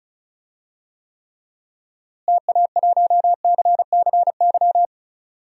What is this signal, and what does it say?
Morse code, heard as a keyed tone, about 36 wpm, 700 Hz. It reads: TA1CCY
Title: DXCC-100 Call Signs - 6 Characters - Repeated 2x - 35wpm